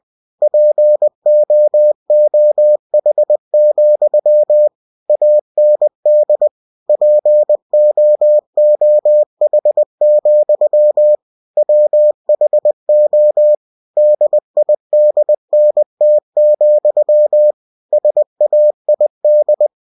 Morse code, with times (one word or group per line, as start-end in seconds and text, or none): 0.4-4.7 POOH,
5.1-6.5 AND
6.9-11.2 POOH,
11.6-13.6 WHO
14.0-17.5 DIDNT,
17.9-19.7 SAID